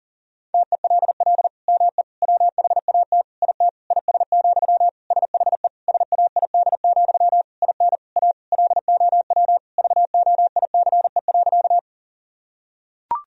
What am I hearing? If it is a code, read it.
Morse code, heard as a keyed tone, about 40 wpm, 700 Hz. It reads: TELL ME WHAT IT IS, SHE SAID, IN A LOW VOICE.
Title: Sentences from Top 1000 Words - Review Part 1 - 40wpm